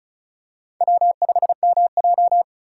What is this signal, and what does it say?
Morse code, heard as a keyed tone, about 36 wpm, 700 Hz. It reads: W5MJ
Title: USA Call Signs - 4 Characters - 2x Character Spacing - 35wpm